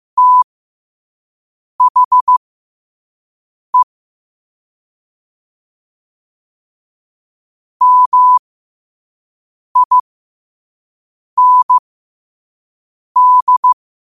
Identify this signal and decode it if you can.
Morse code, heard as a keyed tone, about 15 wpm, spaced as for 5 wpm, 1000 Hz. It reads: THE MIND
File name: qotdfiles / 5wpm